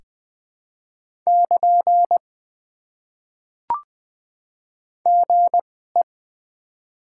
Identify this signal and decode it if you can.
Morse code, heard as a keyed tone, about 20 wpm, spaced as for 14 wpm, 700 Hz. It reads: <KN> GE
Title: Top 90 QSO Elements - 2x Character Spacing - 20wpm